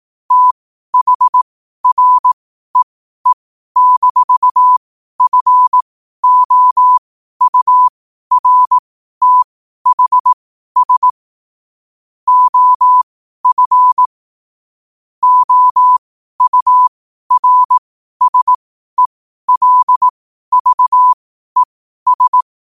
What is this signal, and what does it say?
Morse code, heard as a keyed tone, about 18 wpm, spaced as for 12 wpm, 1000 Hz. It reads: THREE-FOURTHS OF OURSELVES
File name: qotdfiles / 12wpm